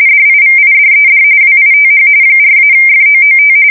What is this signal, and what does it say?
RTTY, about 45.45 baud. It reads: DIVIDED WITH 4-FOOT
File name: fsk170 / USS-New-Jersey